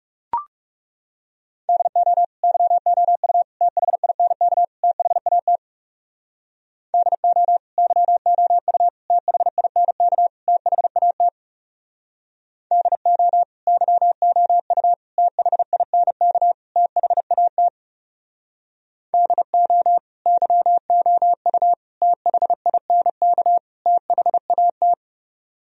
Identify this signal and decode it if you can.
Morse code, heard as a keyed tone, about 40 wpm, 700 Hz. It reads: DO YOU THINK THAT DO YOU THINK THAT DO YOU THINK THAT DO YOU THINK THAT